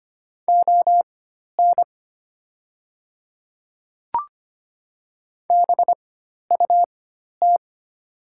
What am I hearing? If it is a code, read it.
Morse code, heard as a keyed tone, about 25 wpm, spaced as for 12 wpm, 700 Hz. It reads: ON BUT
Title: 4x Character Spacing - Top 100 Words - 25wpm